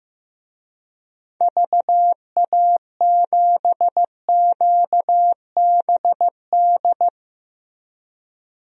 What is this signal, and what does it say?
Morse code, heard as a keyed tone, about 16 wpm, 700 Hz. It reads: VA7QBD
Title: DXCC-100 Call Signs - 6 Characters - Repeated 3x - 15wpm